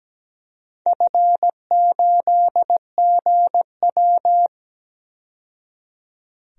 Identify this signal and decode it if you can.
Morse code, heard as a keyed tone, about 17 wpm, 700 Hz. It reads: F8GW
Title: DXCC-100 Call Signs - 4 Characters - 17wpm